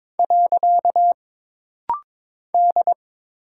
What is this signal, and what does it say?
Morse code, heard as a keyed tone, about 22 wpm, 700 Hz. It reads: . D
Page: This rapid-fire timing is geared to push your ICR (Instant Character Recognition) skills.